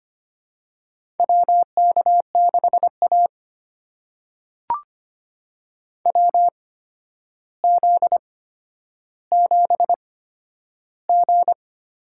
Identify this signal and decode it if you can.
Morse code, heard as a keyed tone, about 25 wpm, spaced as for 11 wpm, 700 Hz. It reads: WK6A WZ7G
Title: USA Call Signs - 4 Characters - 8x Character Spacing - 25wpm